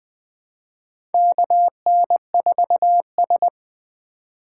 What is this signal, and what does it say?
Morse code, heard as a keyed tone, about 20 wpm, 700 Hz. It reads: KN4S
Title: USA Call Signs - 4 Characters - 4x Character Spacing - 20wpm